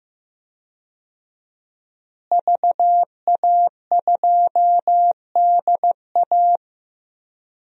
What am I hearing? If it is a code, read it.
Morse code, heard as a keyed tone, about 16 wpm, 700 Hz. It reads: VA2DA
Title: DXCC-100 Call Signs - 5 Characters - Repeated 3x - 15wpm